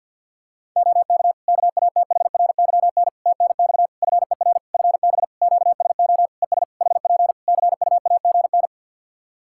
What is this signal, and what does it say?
Morse code, heard as a keyed tone, about 50 wpm, 700 Hz. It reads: OK KATHRYN TNX FER FB QSO ES HP CUAGN
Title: K1USN SST QSO Practice 50wpm